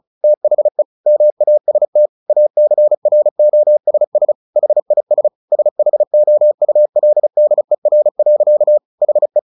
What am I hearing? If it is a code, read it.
Morse code, heard as a keyed tone, about 34 wpm, 600 Hz. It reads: THE MAST ACROSS HIS SHOULDER. HE
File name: the old man and the sea 2 35wpm